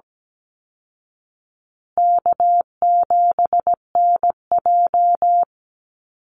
Morse code, heard as a keyed tone, about 17 wpm, 700 Hz. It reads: K7NJ
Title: Call Signs Repeated 3x - 17wpm